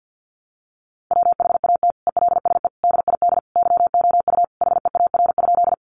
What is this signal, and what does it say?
Morse code, heard as a keyed tone, about 50 wpm, 700 Hz. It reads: WHAT ELSE DID YOU HEAR?
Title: Sentences from Top 800 Words 50wpm